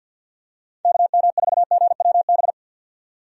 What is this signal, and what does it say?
Morse code, heard as a keyed tone, about 47 wpm, 700 Hz. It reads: KM4GWB
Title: USA Call Signs - 6 Characters - 50wpm